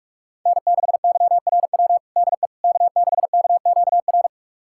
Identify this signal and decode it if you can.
Morse code, heard as a keyed tone, about 46 wpm, 700 Hz. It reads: N6YRW DE K6KQR
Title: USA Callsign DE Callsign - 5 Characters - 45wpm